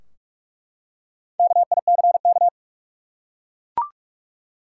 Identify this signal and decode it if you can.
Morse code, heard as a keyed tone, about 43 wpm, 700 Hz. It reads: KICK
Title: Four Letter Words - 45wpm